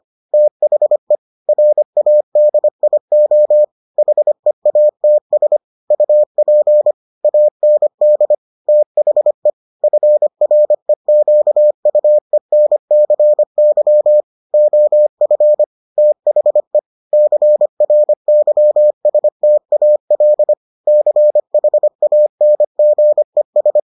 Morse code, heard as a keyed tone, about 25 wpm, 600 Hz. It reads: THE RADIO HEATS UP AND THE FREQUENCY OF THE CRYSTAL CHANGES